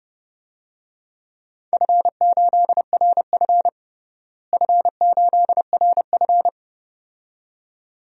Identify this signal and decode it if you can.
Morse code, heard as a keyed tone, about 30 wpm, 700 Hz. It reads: F8RF F8RF
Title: DXCC-100 Call Signs - 4 Characters - Repeated 2x - 30wpm